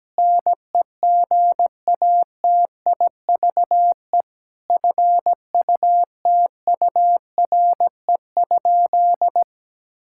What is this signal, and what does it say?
Morse code, heard as a keyed tone, about 17 wpm, 700 Hz. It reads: NEGATIVE FUTURE?
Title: Sentences from Top 1000 Words - Review Part 6 - 17wpm